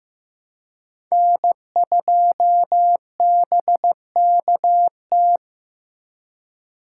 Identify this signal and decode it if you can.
Morse code, heard as a keyed tone, about 15 wpm, 700 Hz. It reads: N2BKT
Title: USA Call Signs - 5 Characters - 2x Character Spacing - 15wpm